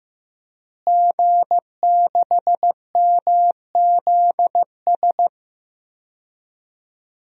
Morse code, heard as a keyed tone, about 15 wpm, 700 Hz. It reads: G6MZS
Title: DXCC-100 Call Signs - 5 Characters - Repeated 2x - 15wpm